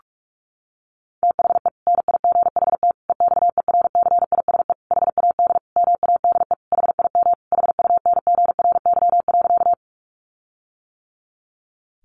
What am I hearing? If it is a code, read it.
Morse code, heard as a keyed tone, about 44 wpm, 700 Hz. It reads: THE NIGHT EXERCISE HAD MADE HIM HUNGRY.
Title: Sentences from Top 1400 Words 45wpm